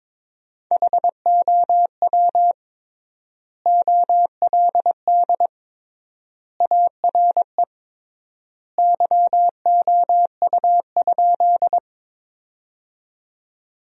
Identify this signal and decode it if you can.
Morse code, heard as a keyed tone, about 22 wpm, 700 Hz. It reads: HOW OLD ARE YOU?
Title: 3x Word-Spacing - Sentences from Top 200 Words - 22wpm